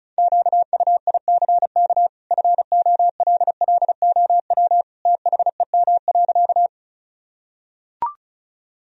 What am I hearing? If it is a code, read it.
Morse code, heard as a keyed tone, about 36 wpm, 700 Hz. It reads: QUICK FOLLOW THEM.
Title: Sentences from Top 1000 Words - Review Part 1 - 35wpm